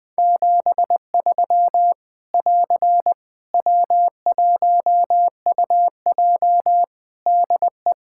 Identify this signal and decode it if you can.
Morse code, heard as a keyed tone, about 21 wpm, 700 Hz. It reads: 73 <AR> W1UJ DE
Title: Practice QSO - Exchange 4 - Ending - 20wpm